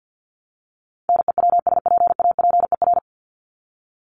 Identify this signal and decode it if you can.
Morse code, heard as a keyed tone, about 50 wpm, 700 Hz. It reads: NEWSPAPER
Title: Top 751-1000 Words 50wpm